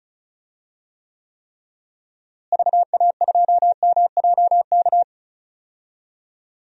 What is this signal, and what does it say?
Morse code, heard as a keyed tone, about 34 wpm, 700 Hz. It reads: VA2MJK
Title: DXCC-100 Call Signs - 6 Characters - Repeated 3x - 35wpm